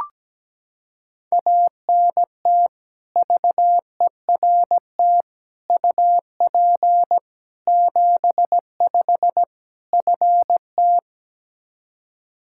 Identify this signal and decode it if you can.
Morse code, heard as a keyed tone, about 17 wpm, 700 Hz. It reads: ANT VERT UP 75 FT
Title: Practice QSO - Exchange 2 - Info - 17wpm